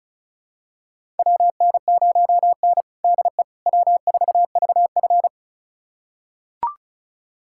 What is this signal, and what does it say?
Morse code, heard as a keyed tone, about 36 wpm, 700 Hz. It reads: WN0N DE W4VF
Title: USA Callsign DE Callsign - 4 Characters - 35wpm